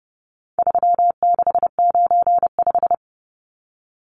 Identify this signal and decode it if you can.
Morse code, heard as a keyed tone, about 29 wpm, 700 Hz. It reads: 3695
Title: Numbers - 4-Digit 30wpm